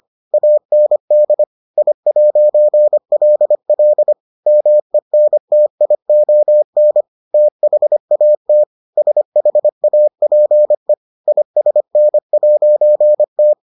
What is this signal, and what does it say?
Morse code, heard as a keyed tone, about 25 wpm, 600 Hz. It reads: AND I'LL MENTION THAT SHAPE ISN'T